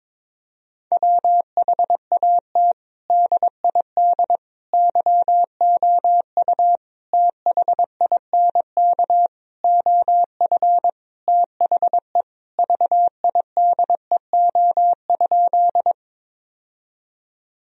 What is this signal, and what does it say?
Morse code, heard as a keyed tone, about 22 wpm, 700 Hz. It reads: WHAT DID YOU THINK OF THE VIDEO?
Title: Sentences from Top 1000 Words - Review Part 4 - 22wpm